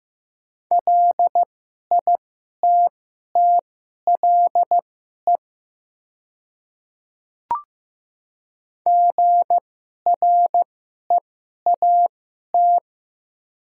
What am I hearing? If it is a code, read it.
Morse code, heard as a keyed tone, about 15 wpm, spaced as for 11 wpm, 700 Hz. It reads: LITTLE GREAT
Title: 2x Character Spacing - Top 200 Words - 15wpm